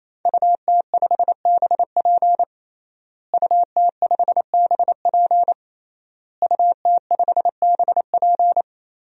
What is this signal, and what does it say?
Morse code, heard as a keyed tone, about 28 wpm, 700 Hz. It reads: UT5BP UT5BP UT5BP
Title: DXCC-100 Call Signs - 5 Characters - Repeated 3x - 28wpm